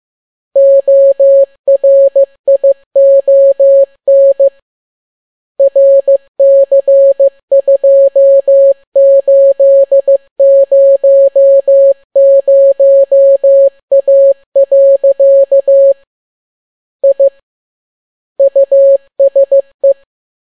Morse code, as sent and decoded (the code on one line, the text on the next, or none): --- .-. .. --- -. / .-. -.-. ..--- ---.. ----- ----- .- .-.-.- / .. / ..- ... .
ORION RC2800A. I USE